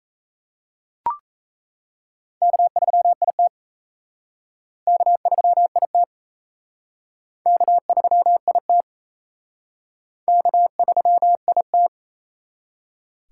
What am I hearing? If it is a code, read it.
Morse code, heard as a keyed tone, about 34 wpm, 700 Hz. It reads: K3IT K3IT K3IT K3IT
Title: Speed Racing - USA 1x2 Call Signs 28wpm